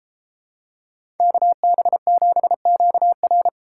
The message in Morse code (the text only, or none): KB7QR